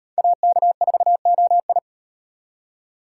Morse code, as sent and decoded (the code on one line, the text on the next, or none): .- -.- ....- --- ..
AK4OI